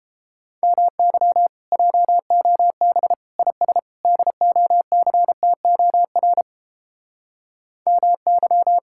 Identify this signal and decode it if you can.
Morse code, heard as a keyed tone, about 32 wpm, 700 Hz. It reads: MY JOB IS DOCTOR MY